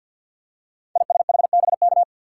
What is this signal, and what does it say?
Morse code, heard as a keyed tone, about 48 wpm, 700 Hz. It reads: ISHBK
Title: Code Groups - 5 Letters - 50wpm